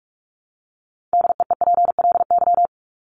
Morse code, heard as a keyed tone, about 44 wpm, 700 Hz. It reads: DEEPLY
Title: LY Suffix - Top 100 Words with Base Words 45wpm